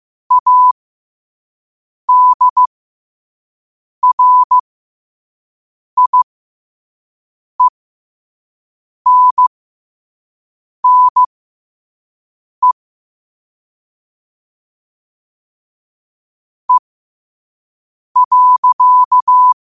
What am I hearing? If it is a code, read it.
Morse code, heard as a keyed tone, about 15 wpm, spaced as for 5 wpm, 1000 Hz. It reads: ADRIENNE E.
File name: qotdfiles / 5wpm